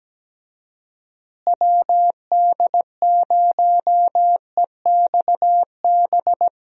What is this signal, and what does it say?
Morse code, heard as a keyed tone, about 17 wpm, 700 Hz. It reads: WD0EXB
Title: USA Call Signs - 6 Characters - Repeated 2x - 17wpm